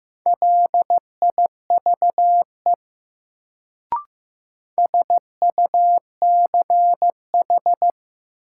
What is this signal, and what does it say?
Morse code, heard as a keyed tone, about 15 wpm, 700 Hz. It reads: LIVE SUCH
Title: Sentences from Top 200 Words - New Word Drill - 15wpm